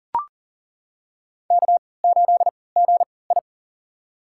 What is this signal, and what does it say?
Morse code, heard as a keyed tone, about 38 wpm, spaced as for 22 wpm, 700 Hz. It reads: K8GI
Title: USA Call Signs - 4 Characters - 3x Character Spacing - 40wpm